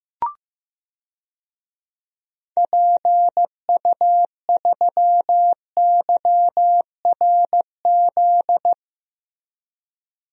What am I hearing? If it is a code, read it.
Morse code, heard as a keyed tone, about 15 wpm, 700 Hz. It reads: PU3YRZ
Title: DXCC-100 Call Signs - 6 Characters - Repeated 3x - 15wpm